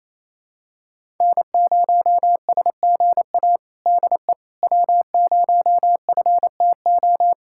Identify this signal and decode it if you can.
Morse code, heard as a keyed tone, about 28 wpm, 700 Hz. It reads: N0SGA DE W0FTO
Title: USA Callsign DE Callsign - 5 Characters - 28wpm